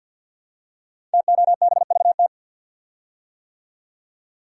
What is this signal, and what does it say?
Morse code, heard as a keyed tone, about 50 wpm, 700 Hz. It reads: TOBVT
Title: Code Groups - 5 Letters - 50wpm